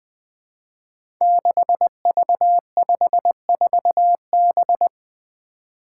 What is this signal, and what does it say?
Morse code, heard as a keyed tone, about 20 wpm, 700 Hz. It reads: 6V54B
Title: Five Challenging Letter-Number Combinations 20wpm